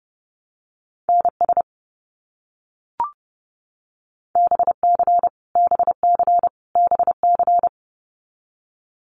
Morse code, heard as a keyed tone, about 32 wpm, 700 Hz. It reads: NS BC BC BC
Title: Canadian Provinces and Territories - Repeated 3x - 30wpm